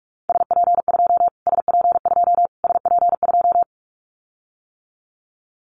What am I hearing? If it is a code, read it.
Morse code, heard as a keyed tone, about 44 wpm, 700 Hz. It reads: SP2 SP2 SP2
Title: Callsign Prefixes for Top 25 DXCC - Repeated 3x - 45wpm